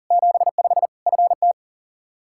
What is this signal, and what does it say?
Morse code, heard as a keyed tone, about 40 wpm, 700 Hz. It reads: 75 FT
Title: K1USN SST QSO Practice 40wpm